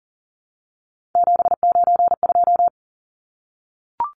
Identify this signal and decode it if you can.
Morse code, heard as a keyed tone, about 40 wpm, 700 Hz. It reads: 792
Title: Numbers - 3-Digit 40wpm